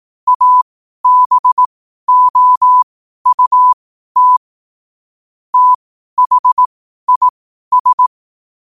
Morse code, heard as a keyed tone, about 18 wpm, spaced as for 12 wpm, 1000 Hz. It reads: ABOUT THIS
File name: qotdfiles / 12wpm